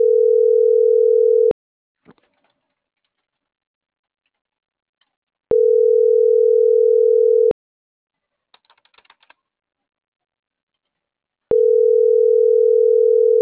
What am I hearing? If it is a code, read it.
Ringback tone (North America), 440 + 480 Hz.